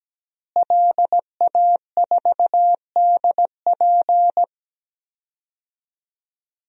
Morse code, heard as a keyed tone, about 17 wpm, 700 Hz. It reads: LA4DP
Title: DXCC-100 Call Signs - 5 Characters - Repeated 2x - 17wpm